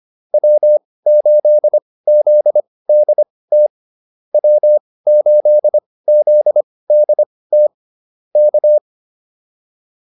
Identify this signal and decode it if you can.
Morse code, heard as a keyed tone, about 25 wpm, spaced as for 18 wpm, 600 Hz. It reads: W8ZDT W8ZDT K